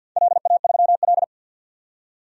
Morse code, heard as a keyed tone, about 46 wpm, 700 Hz. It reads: LA3L
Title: DXCC-100 Call Signs - 4 Characters - 50wpm